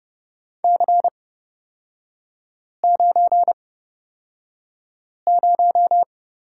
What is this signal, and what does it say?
Morse code, heard as a keyed tone, about 29 wpm, spaced as for 5 wpm, 700 Hz. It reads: C90
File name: Lesson 36e - BT - Three Characters 30wpm